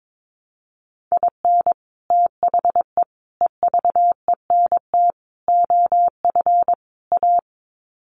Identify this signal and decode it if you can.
Morse code, heard as a keyed tone, about 22 wpm, 700 Hz. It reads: IN THE EVENT OF A
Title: Sets of 5 Words Encore 22wpm